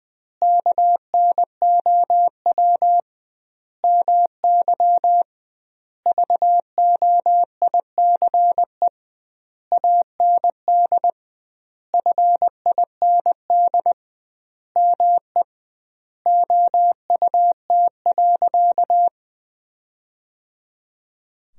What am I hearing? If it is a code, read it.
Morse code, heard as a keyed tone, about 21 wpm, 700 Hz. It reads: KNOW MY VOICE AND FIND ME OUT.
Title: 2x Word-Spacing - Sentences from Top 500 Words - 20wpm